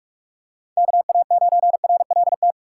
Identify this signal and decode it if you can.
Morse code, heard as a keyed tone, about 46 wpm, 700 Hz. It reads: KA9RLT
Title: USA Call Signs - 3x Character Spacing - 45wpm